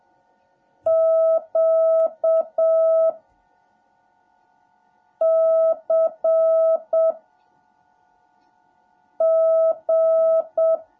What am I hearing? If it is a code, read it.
Morse code, heard as a keyed tone, about 7 wpm, 650 Hz. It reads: Q C G
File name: jh8cbh / 2021cw / cw17a